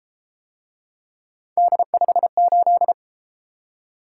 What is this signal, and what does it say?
Morse code, heard as a keyed tone, about 32 wpm, 700 Hz. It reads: D58